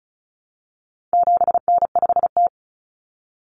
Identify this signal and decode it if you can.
Morse code, heard as a keyed tone, about 34 wpm, 700 Hz. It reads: 7N5T
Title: Common Cut Numbers - 4-Digit 35wpm